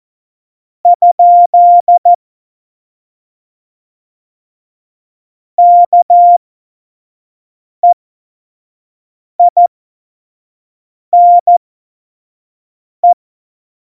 Morse code, heard as a keyed tone, about 14 wpm, spaced as for 5 wpm, 700 Hz. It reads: ? KEINE
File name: Morsekurs / Witze / Witz 02. April 2019, 25 F70